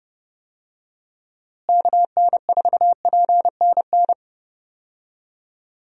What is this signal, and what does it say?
Morse code, heard as a keyed tone, about 32 wpm, 700 Hz. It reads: KN4PNN
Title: USA Call Signs - 6 Characters - Repeated 3x - 30wpm